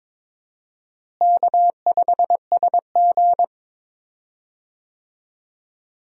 Morse code, heard as a keyed tone, about 22 wpm, 700 Hz. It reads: K5SG
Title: USA Call Signs - 4 Characters - Repeated 3x - 22wpm